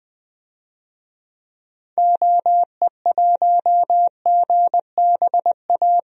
Morse code, heard as a keyed tone, about 20 wpm, 700 Hz. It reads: OE1GBA
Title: DXCC-100 Call Signs - 6 Characters - Repeated 3x - 20wpm